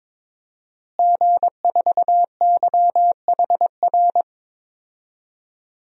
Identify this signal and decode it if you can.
Morse code, heard as a keyed tone, about 22 wpm, 700 Hz. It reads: G4YHR
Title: DXCC-100 Call Signs - 5 Characters - Repeated 3x - 22wpm